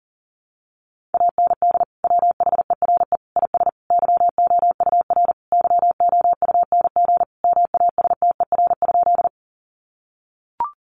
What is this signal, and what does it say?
Morse code, heard as a keyed tone, about 37 wpm, 700 Hz. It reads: AND WHERE IS YOUR YOUNG MASTER?
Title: Sentences from Top 1300 Words 40wpm